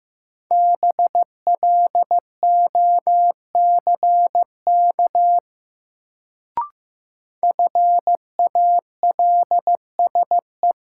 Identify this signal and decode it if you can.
Morse code, heard as a keyed tone, about 15 wpm, 700 Hz. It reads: BLOCK FALSE
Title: Sentences from Top 1600 Words - New Word Drill Alt - 15wpm